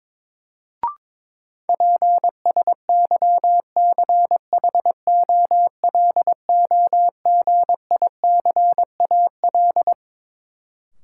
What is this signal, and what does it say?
Morse code, heard as a keyed tone, about 22 wpm, 700 Hz. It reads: PSYCHOLOGICAL
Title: Sentences from Top 1400 Words - New Word Drill - 22wpm